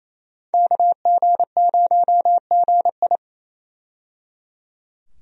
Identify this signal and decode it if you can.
Morse code, heard as a keyed tone, about 28 wpm, 700 Hz. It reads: KG0GI
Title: USA Call Signs - 5 Characters - 28wpm